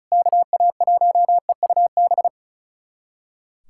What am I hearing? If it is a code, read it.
Morse code, heard as a keyed tone, about 36 wpm, 700 Hz. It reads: KA1EUB
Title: USA Callsign DE Callsign 35wpm